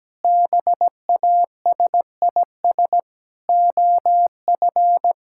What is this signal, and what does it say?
Morse code, heard as a keyed tone, about 17 wpm, 700 Hz. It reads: BASIS OF
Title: Sentences from Top 1000 Words - Review Part 5 - 17wpm